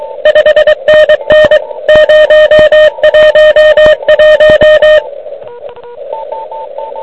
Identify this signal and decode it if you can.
Morse code, heard as a keyed tone, about 24 wpm, 1850 Hz, with radio noise. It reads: 5NN 011
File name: ARCHIV / QCX / QCX Pileup